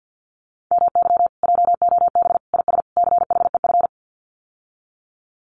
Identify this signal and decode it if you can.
Morse code, heard as a keyed tone, about 50 wpm, 700 Hz. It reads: MY JOB IS CHEF
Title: Common QSO Phrases Encore 50wpm